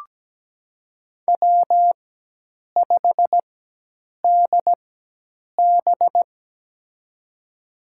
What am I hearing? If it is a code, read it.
Morse code, heard as a keyed tone, about 17 wpm, spaced as for 8 wpm, 700 Hz. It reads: W5DB
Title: USA Call Signs - 4 Characters - 4x Character Spacing - 17wpm